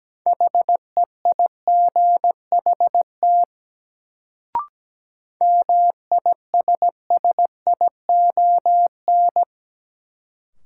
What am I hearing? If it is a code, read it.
Morse code, heard as a keyed tone, about 17 wpm, 700 Hz. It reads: HEIGHT MISSION
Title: Sentences from Top 1600 Words - New Word Drill - 17wpm